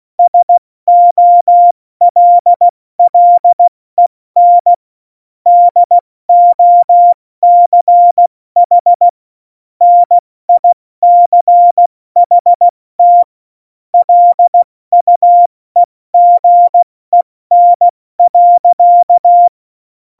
Morse code, heard as a keyed tone, about 16 wpm, 700 Hz. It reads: SOLLEN DOCH NICHT LUEGEN.
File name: Morsekurs / Witze / Witz 05. Januar 2021, 70 F80